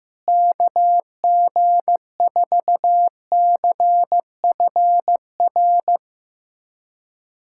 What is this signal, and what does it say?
Morse code, heard as a keyed tone, about 15 wpm, 700 Hz. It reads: KG4CFR
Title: USA Callsign DE Callsign - 6 Characters - 15wpm